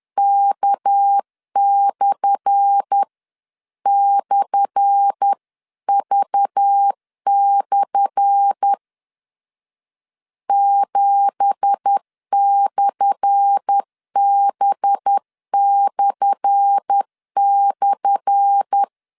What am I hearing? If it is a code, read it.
Morse code, heard as a keyed tone, about 10 wpm, 800 Hz. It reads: K//V/ 7/B//
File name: LEZIONI / 10 SIGNS / 10A SLASH 10wpm 2808